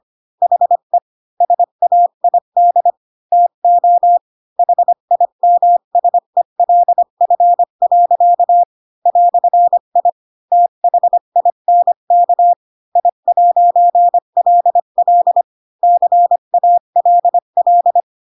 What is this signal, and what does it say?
Morse code, heard as a keyed tone, about 25 wpm, 700 Hz. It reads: HE SAID TO HIMSELF. "I THINK I'LL CALL
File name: Pinocchio / Pinocchio ch3 20 25 ik1hge